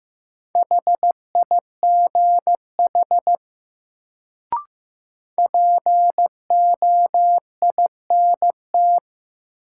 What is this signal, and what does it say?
Morse code, heard as a keyed tone, about 15 wpm, 700 Hz. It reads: HIGH POINT
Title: Sentences from Top 200 Words - New Word Drill - 15wpm